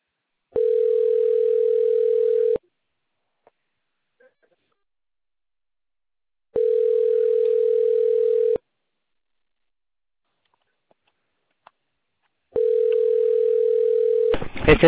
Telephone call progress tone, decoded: ringback tone